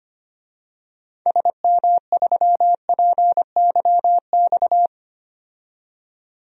Morse code, heard as a keyed tone, about 25 wpm, 700 Hz. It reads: SM3PYX
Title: DXCC-100 Call Signs - 6 Characters - 25wpm